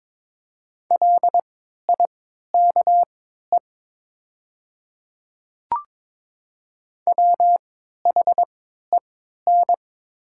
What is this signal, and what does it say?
Morse code, heard as a keyed tone, about 22 wpm, spaced as for 12 wpm, 700 Hz. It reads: LIKE WHEN
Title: 3x Character Spacing - Top 100 Words - 22wpm